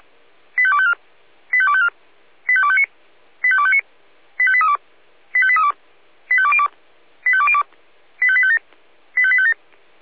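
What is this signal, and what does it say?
Five-tone selective call (ZVEI), heard as quick runs of five tones, tones of about 80 ms. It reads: 85254 85254 85259 85259 85832 85832 85202 85202 85857 85857
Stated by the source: ZVEI five-tone call 85850 received by SDR.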